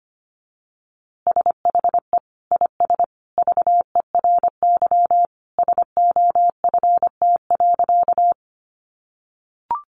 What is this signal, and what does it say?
Morse code, heard as a keyed tone, about 25 wpm, 700 Hz. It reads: SHE IS VERY SOFT.
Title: Easy Phrases 25wpm